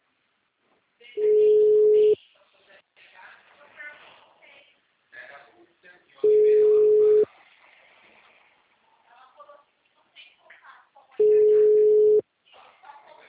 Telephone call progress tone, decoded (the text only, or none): ringback tone